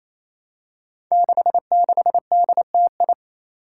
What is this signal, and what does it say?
Morse code, heard as a keyed tone, about 27 wpm, 700 Hz. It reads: 66DTI
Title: Five Challenging Letter-Number Combinations 28wpm